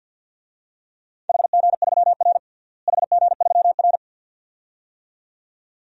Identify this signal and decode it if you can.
Morse code, heard as a keyed tone, about 50 wpm, 700 Hz. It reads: HG3R HG3R
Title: DXCC-100 Call Signs - 4 Characters - Repeated 2x - 50wpm